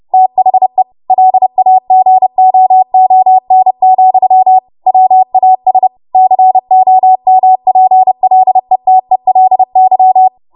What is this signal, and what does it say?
Morse code, heard as a keyed tone, about 30 wpm, 750 Hz. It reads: THE LAGOON, WAS COMPLETELY
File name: Morse / 30WPM